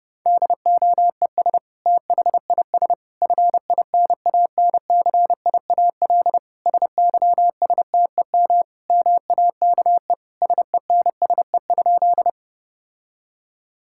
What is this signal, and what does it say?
Morse code, heard as a keyed tone, about 30 wpm, 700 Hz. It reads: DOES THIS FINANCIAL SYSTEM MAKE SENSE?
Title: Sentences from Top 1000 Words - Review Part 5 - 30wpm